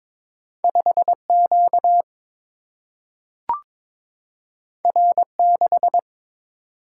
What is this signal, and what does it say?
Morse code, heard as a keyed tone, about 22 wpm, 700 Hz. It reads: 5Q R6
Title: Single-Letter Single-Number 22wpm